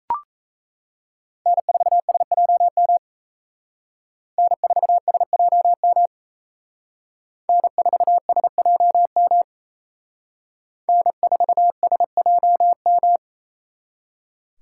Morse code, heard as a keyed tone, about 36 wpm, 700 Hz. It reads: N4SJM N4SJM N4SJM N4SJM